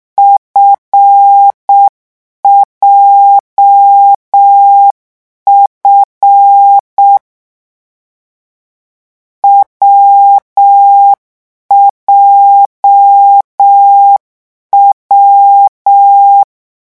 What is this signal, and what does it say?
Morse code, heard as a keyed tone, about 6 wpm, 800 Hz. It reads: FJF WJW
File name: morse mp3 / lecon7-vitesse 6